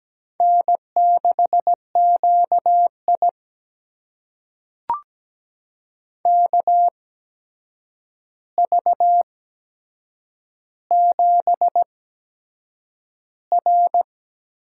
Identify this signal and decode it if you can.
Morse code, heard as a keyed tone, about 17 wpm, spaced as for 7 wpm, 700 Hz. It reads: N6QI KV7R
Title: USA Call Signs - 4 Characters - 8x Character Spacing - 17wpm